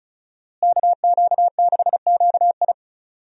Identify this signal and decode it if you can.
Morse code, heard as a keyed tone, about 36 wpm, 700 Hz. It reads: KQ6QI
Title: USA Call Signs - 5 Characters - Repeated 2x - 35wpm